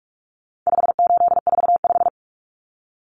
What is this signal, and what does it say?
Morse code, heard as a keyed tone, about 44 wpm, 700 Hz. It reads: 5845
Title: Common Cut Numbers - 4-Digit 45wpm